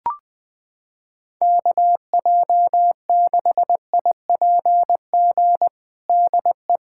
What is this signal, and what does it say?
Morse code, heard as a keyed tone, about 20 wpm, 700 Hz. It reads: KJ6IPG DE
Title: USA Callsign DE Callsign 20wpm